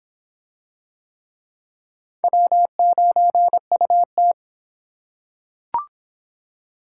Morse code, heard as a keyed tone, about 26 wpm, 700 Hz. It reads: W9UT